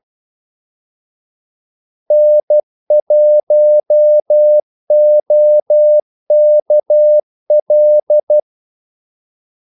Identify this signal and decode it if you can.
Morse code, heard as a keyed tone, about 12 wpm, 600 Hz. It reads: N1OKL
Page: Kaldesignaler 12 wpm